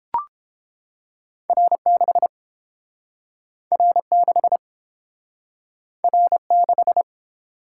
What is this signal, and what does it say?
Morse code, heard as a keyed tone, about 31 wpm, 700 Hz. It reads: R6 R6 R6